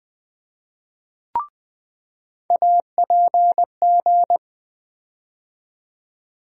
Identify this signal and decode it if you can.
Morse code, heard as a keyed tone, about 20 wpm, 700 Hz. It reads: APG
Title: USA Call Sign Postfixes - 3 Letters 20wpm